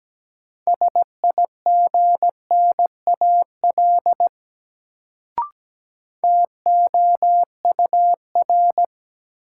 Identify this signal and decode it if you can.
Morse code, heard as a keyed tone, about 17 wpm, 700 Hz. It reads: SIGNAL TOUR
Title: Sentences from Top 1400 Words - New Word Drill Alt - 17wpm